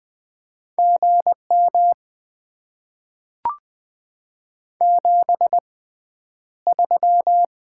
Morse code, heard as a keyed tone, about 20 wpm, spaced as for 10 wpm, 700 Hz. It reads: GM 73
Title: Top 20 QSO Elements - 6x Character Spacing - 20wpm